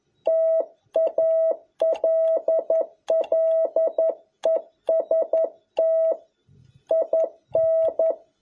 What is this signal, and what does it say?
Morse code, heard as a keyed tone, about 11 wpm, 650 Hz. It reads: TALLEST IN
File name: jh8cbh / 2021cw